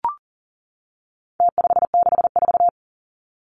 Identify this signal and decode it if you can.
Morse code, heard as a keyed tone, about 40 wpm, 700 Hz. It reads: T564
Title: Numbers - Contest Style - 4-Digit 40wpm